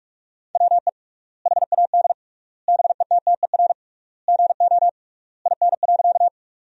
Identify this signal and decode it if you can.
Morse code, heard as a keyed tone, about 44 wpm, 700 Hz. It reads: WE HAD BETTER GO IN.
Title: 3x Word-Spacing - Sentences from Top 400 Words - 45wpm